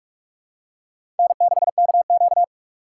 Morse code, heard as a keyed tone, about 42 wpm, 700 Hz. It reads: N6KQ
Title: USA Call Signs - 2x Character Spacing - 45wpm